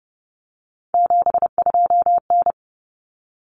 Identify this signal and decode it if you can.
Morse code, heard as a keyed tone, about 30 wpm, 700 Hz. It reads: 72N
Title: Common Cut Numbers - 3-Digit 30wpm